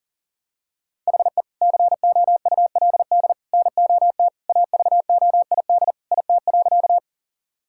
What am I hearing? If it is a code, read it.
Morse code, heard as a keyed tone, about 40 wpm, 700 Hz. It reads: HE COULD NOT AVOID IT.
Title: Sentences from Top 1000 Words - Review Part 6 - 40wpm